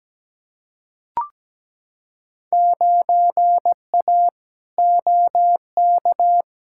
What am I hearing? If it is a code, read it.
Morse code, heard as a keyed tone, about 17 wpm, 700 Hz. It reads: 9A OK
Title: Field Day Exchanges Encore - 17wpm